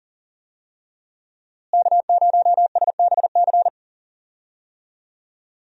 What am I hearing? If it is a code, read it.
Morse code, heard as a keyed tone, about 40 wpm, 700 Hz. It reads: K0SBC
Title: USA Call Signs - 5 Characters - Repeated 3x - 40wpm